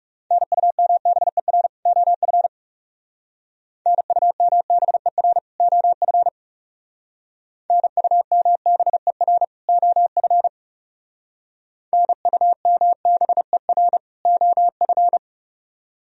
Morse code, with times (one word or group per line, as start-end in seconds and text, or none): 0.3-1.7 NUMBER
1.9-2.5 OF
3.9-5.4 NUMBER
5.6-6.3 OF
7.7-9.4 NUMBER
9.7-10.5 OF
11.9-14.0 NUMBER
14.3-15.2 OF